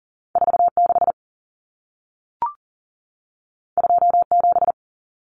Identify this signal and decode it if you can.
Morse code, heard as a keyed tone, about 40 wpm, 700 Hz. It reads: 46 27
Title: Numbers - 2-Digit 40wpm